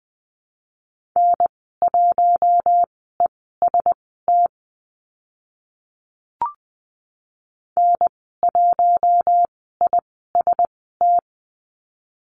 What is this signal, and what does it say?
Morse code, heard as a keyed tone, about 20 wpm, spaced as for 15 wpm, 700 Hz. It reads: N1EST N1IST
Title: 042 Lesson 8g - Number 1 - Call Signs 20wpm